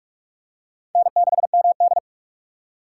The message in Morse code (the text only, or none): N6MD